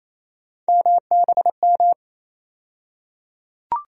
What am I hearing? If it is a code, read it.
Morse code, heard as a keyed tone, about 28 wpm, 700 Hz. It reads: MBM
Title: USA Call Sign Postfixes - 3 Letters 28wpm